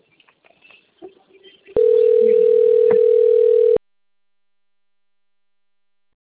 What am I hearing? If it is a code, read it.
Ringback tone (North America), 440 + 480 Hz.